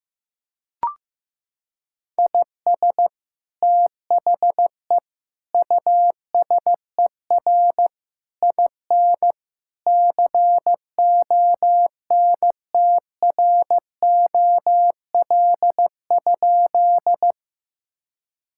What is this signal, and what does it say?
Morse code, heard as a keyed tone, about 15 wpm, 700 Hz. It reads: IS THE USER IN CONTROL?
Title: Sentences from Top 1000 Words - Review Part 5 - 15wpm